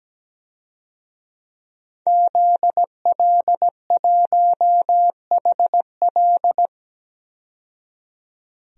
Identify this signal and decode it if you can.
Morse code, heard as a keyed tone, about 17 wpm, 700 Hz. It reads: ZL1HL
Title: DXCC-100 Call Signs - 5 Characters - 17wpm